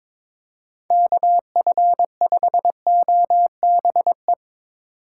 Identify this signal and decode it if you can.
Morse code, heard as a keyed tone, about 22 wpm, 700 Hz. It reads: KF5OBE
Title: USA Call Signs - 6 Characters - 2x Character Spacing - 22wpm